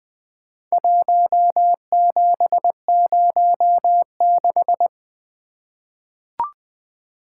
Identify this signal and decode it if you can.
Morse code, heard as a keyed tone, about 20 wpm, 700 Hz. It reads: 1706
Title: Four Letter-Number Combinations - 20wpm